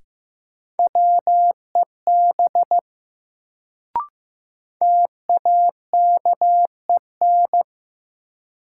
Morse code, heard as a keyed tone, about 15 wpm, 700 Hz. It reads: WEB TAKEN
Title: Sentences from Top 1000 Words - Review Part 4 - New Word Drill - 15wpm